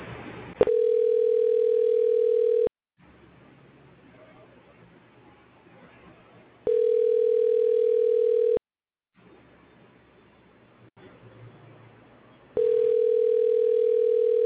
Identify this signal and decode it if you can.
Ringback tone (North America), 440 + 480 Hz.